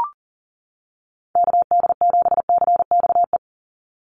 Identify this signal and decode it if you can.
Morse code, heard as a keyed tone, about 40 wpm, 700 Hz. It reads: KD7CXE
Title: USA Call Signs 40wpm